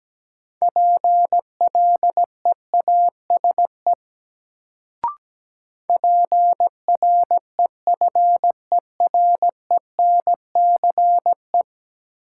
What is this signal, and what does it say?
Morse code, heard as a keyed tone, about 17 wpm, 700 Hz. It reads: PLEASE PREFERENCE
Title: Sentences from Top 1500 Words - New Word Drill Alt - 17wpm